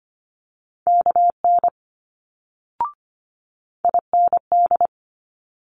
Morse code, heard as a keyed tone, about 25 wpm, 700 Hz. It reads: KN IND
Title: Multi-letter Phonemes Quick Fire 25wpm